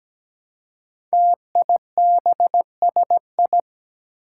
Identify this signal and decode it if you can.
Morse code, heard as a keyed tone, about 17 wpm, 700 Hz. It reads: TIBSI
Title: Five Challenging Letter-Number Combinations 17wpm